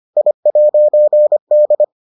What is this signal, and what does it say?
Morse code, heard as a keyed tone, about 25 wpm, 600 Hz. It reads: I'D